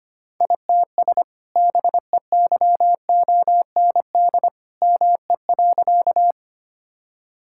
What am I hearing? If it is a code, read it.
Morse code, heard as a keyed tone, about 25 wpm, 700 Hz. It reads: ITS BEYOND ME.
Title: Sentences from Top 1000 Words - Review Part 6 - 25wpm